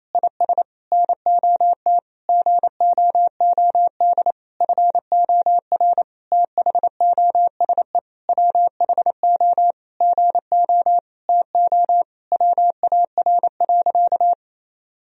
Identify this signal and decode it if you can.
Morse code, heard as a keyed tone, about 28 wpm, 700 Hz. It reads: IS NOT GOOD FOR THOSE WHO GO TO WAR.
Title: Sentences from Top 1000 Words - Review Part 2 - 28wpm